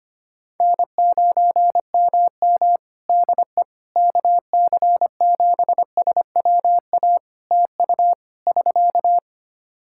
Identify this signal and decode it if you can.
Morse code, heard as a keyed tone, about 25 wpm, 700 Hz. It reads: N9MM DE KC7SWA TU <SK>
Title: Practice QSO - Exchange 4 - Ending - 25wpm